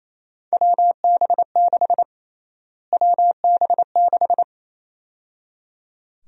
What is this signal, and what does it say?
Morse code, heard as a keyed tone, about 28 wpm, 700 Hz. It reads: WB6 WB6
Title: USA Call Sign Prefixes - Repeated 2x - 28wpm